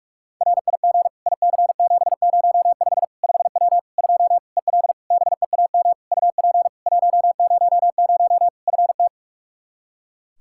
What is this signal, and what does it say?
Morse code, heard as a keyed tone, about 46 wpm, 700 Hz. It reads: RIG IC705 5W 2 EL BEAM UP 100 FT
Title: K1USN SST QSO Practice 45wpm